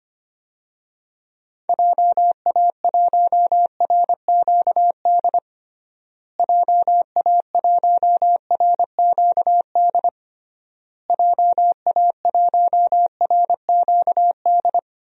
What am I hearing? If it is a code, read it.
Morse code, heard as a keyed tone, about 25 wpm, 700 Hz. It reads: JA1RQD JA1RQD JA1RQD
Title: DXCC-100 Call Signs - 6 Characters - Repeated 3x - 25wpm